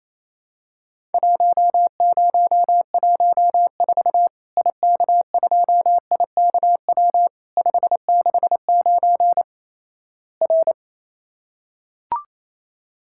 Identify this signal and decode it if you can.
Morse code, heard as a keyed tone, about 28 wpm, 700 Hz. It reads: 1014 IK2IKW 569 R
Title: Worked All Europe DX Contest - QTC Exchange - 28wpm